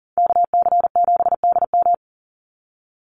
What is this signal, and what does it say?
Morse code, heard as a keyed tone, about 40 wpm, 700 Hz. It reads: KC7DM
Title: NA SOTA Activator Callsigns 40wpm